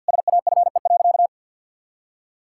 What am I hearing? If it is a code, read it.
Morse code, heard as a keyed tone, about 46 wpm, 700 Hz. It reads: SAFE.
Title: Sentences from Top 1600 Words 50wpm